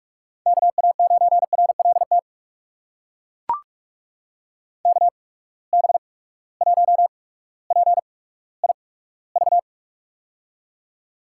Morse code, heard as a keyed tone, about 42 wpm, spaced as for 19 wpm, 700 Hz. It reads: KA9RLT KB1PIV
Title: USA Call Signs - 6 Characters - 8x Character Spacing - 45wpm